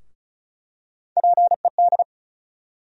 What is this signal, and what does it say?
Morse code, heard as a keyed tone, about 34 wpm, 700 Hz. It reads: PED
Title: USA Call Sign Postfixes - 3 Letters 35wpm